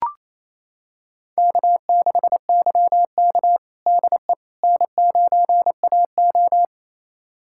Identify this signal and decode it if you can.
Morse code, heard as a keyed tone, about 28 wpm, 700 Hz. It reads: K6YK DE N9AO
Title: USA Callsign DE Callsign - 4 Characters - 28wpm